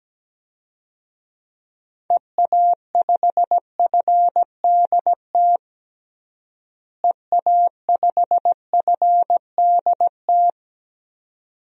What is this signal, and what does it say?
Morse code, heard as a keyed tone, about 17 wpm, 700 Hz. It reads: EA5FDT EA5FDT
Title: DXCC-100 Call Signs - 6 Characters - Repeated 2x - 17wpm